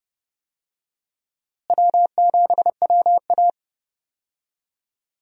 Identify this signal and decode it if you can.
Morse code, heard as a keyed tone, about 30 wpm, 700 Hz. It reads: W7WA
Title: USA Call Signs - 4 Characters - Repeated 3x - 30wpm